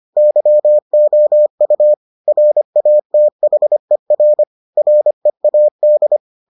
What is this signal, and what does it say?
Morse code, heard as a keyed tone, about 25 wpm, 600 Hz. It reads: YOU RATHER READ